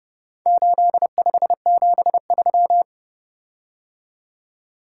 Morse code, heard as a keyed tone, about 30 wpm, 700 Hz. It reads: 8573